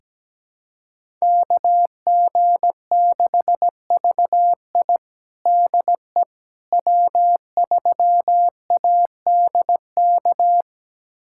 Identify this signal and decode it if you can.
Morse code, heard as a keyed tone, about 17 wpm, 700 Hz. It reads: KG6VI DE W3ADK
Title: USA Callsign DE Callsign - 5 Characters - 17wpm